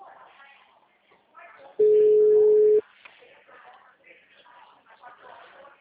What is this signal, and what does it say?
Ringback tone (Europe, 425 Hz), with speech.